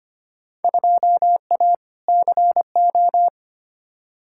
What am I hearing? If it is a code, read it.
Morse code, heard as a keyed tone, about 25 wpm, 700 Hz. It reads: 2A CO
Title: Field Day Exchanges - 25wpm